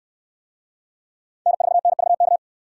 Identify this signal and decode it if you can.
Morse code, heard as a keyed tone, about 69 wpm, 700 Hz. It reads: N4NVK